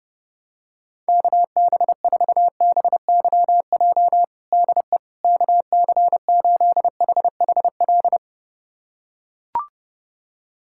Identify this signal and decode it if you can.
Morse code, heard as a keyed tone, about 32 wpm, 700 Hz. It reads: KB4BYJ DE KC8HHL
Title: USA Callsign DE Callsign 30wpm